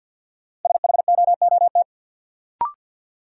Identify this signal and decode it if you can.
Morse code, heard as a keyed tone, about 46 wpm, 700 Hz. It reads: SHOOT
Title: Sentences from Top 1200 Words - New Word Drill - 50wpm